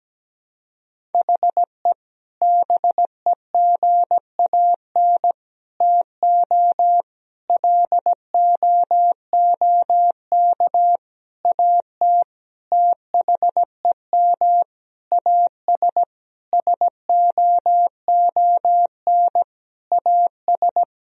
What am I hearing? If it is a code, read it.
Morse code, heard as a keyed tone, about 17 wpm, 700 Hz. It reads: HE BEGAN TO LOOK AT THEM AS SOON AS
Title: Sentences from Top 1000 Words - Review Part 1 - 17wpm